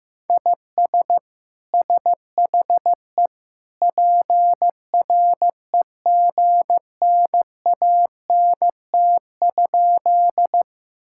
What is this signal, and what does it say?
Morse code, heard as a keyed tone, about 15 wpm, 700 Hz. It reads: IS SHE PREGNANT?
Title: Sentences from Top 1000 Words - Review Part 6 - 15wpm